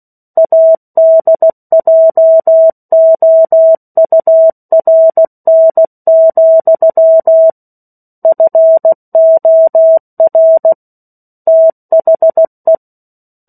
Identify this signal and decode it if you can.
Morse code, heard as a keyed tone, about 16 wpm, 650 Hz. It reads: ADJOURN, FOR THE
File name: 03-Chapter-03 16 WPM